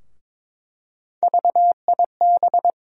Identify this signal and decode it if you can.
Morse code, heard as a keyed tone, about 22 wpm, 700 Hz. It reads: VIB
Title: USA Call Sign Postfixes - 3 Letters 22wpm